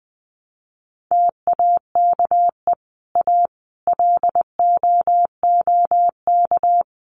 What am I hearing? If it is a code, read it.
Morse code, heard as a keyed tone, about 20 wpm, 700 Hz. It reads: TAKE A LOOK
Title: Sets of 4 Words 20wpm